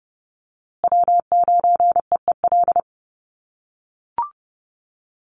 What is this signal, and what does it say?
Morse code, heard as a keyed tone, about 30 wpm, 700 Hz. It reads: W9EEL
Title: USA Call Signs Repeated 2x 30wpm